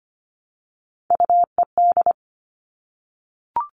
Morse code, heard as a keyed tone, about 25 wpm, 700 Hz. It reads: UED
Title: All Cut Numbers - 3-Digit 25wpm